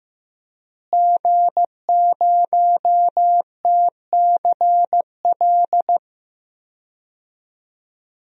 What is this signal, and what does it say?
Morse code, heard as a keyed tone, about 15 wpm, 700 Hz. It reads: G0TCL
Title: DXCC-100 Call Signs - 5 Characters - Repeated 2x - 15wpm